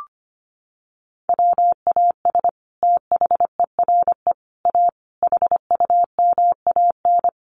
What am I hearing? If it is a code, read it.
Morse code, heard as a keyed tone, about 25 wpm, 700 Hz. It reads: WAS THERE A HUMAN
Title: Sentences from Top 1300 Words 25wpm